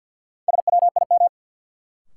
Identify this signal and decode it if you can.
Morse code, heard as a keyed tone, about 52 wpm, 700 Hz. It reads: SWIM
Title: Sentences from Top 1000 Words - Review Part 6 - New Word Drill - 50wpm